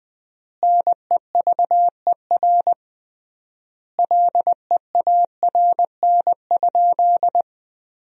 Morse code, heard as a keyed tone, about 21 wpm, 700 Hz. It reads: NEVER LEARN?
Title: 3x Word-Spacing - Sentences from Top 300 Words - 20wpm